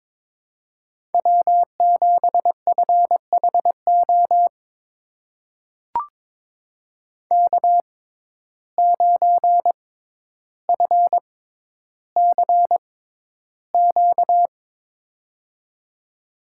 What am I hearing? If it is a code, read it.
Morse code, heard as a keyed tone, about 22 wpm, spaced as for 11 wpm, 700 Hz. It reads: W7FHO K9FCQ
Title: USA Call Signs - 5 Characters - 6x Character Spacing - 22wpm